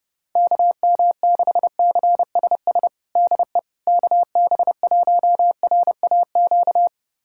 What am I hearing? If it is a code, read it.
Morse code, heard as a keyed tone, about 30 wpm, 700 Hz. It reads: KM6CSS DE KB1RAQ
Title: USA Callsign DE Callsign 30wpm